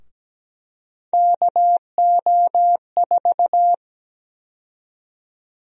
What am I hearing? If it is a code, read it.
Morse code, heard as a keyed tone, about 17 wpm, 700 Hz. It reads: KO4
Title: USA Call Sign Prefixes - 17wpm